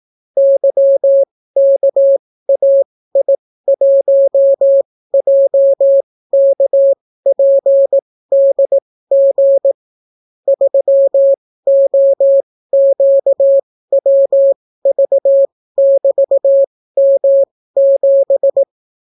Morse code, heard as a keyed tone, about 18 wpm, spaced as for 15 wpm, 550 Hz. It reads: YKAI1JKPDG 3OQWV<BT>M7